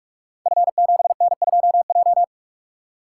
Morse code, heard as a keyed tone, about 44 wpm, 690 Hz. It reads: F7N2J
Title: Code Groups - 5 Letter:Numbers - 45wpm